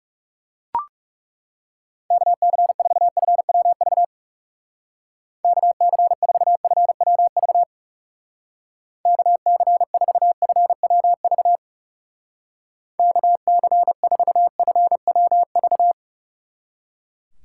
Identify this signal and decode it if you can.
Morse code, heard as a keyed tone, about 36 wpm, 700 Hz. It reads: KC4FWV KC4FWV KC4FWV KC4FWV
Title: Speed Racing - USA Call Signs 30wpm